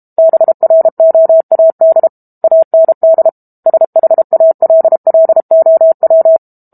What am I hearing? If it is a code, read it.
Morse code, heard as a keyed tone, about 32 wpm, 650 Hz. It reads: BROAD AND SHALLOW
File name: Book Two / 06-Chapter 6 - 32 WPM